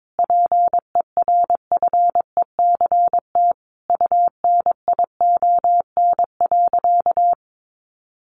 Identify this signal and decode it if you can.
Morse code, heard as a keyed tone, about 22 wpm, 700 Hz. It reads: PERFECT UNION.
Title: Sentences from Top 1400 Words 22wpm